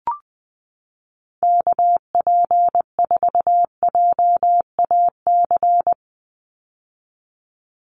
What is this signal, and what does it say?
Morse code, heard as a keyed tone, about 20 wpm, 700 Hz. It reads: KP4JAC
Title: USA Call Signs 20wpm